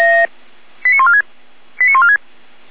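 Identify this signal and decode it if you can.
Five-tone selective call (ZVEI), heard as quick runs of five tones, tones of about 70 ms. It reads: 89146 89146
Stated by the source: ZVEI five-tone call 89299 received on an SDR.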